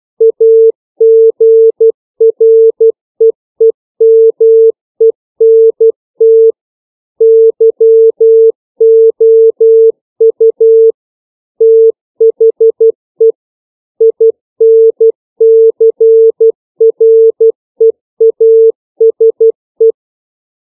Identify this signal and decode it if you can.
Morse code, heard as a keyed tone, about 12 wpm, 450 Hz. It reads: AGREEMENT YOU THE INCREASE
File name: CW / five common words 2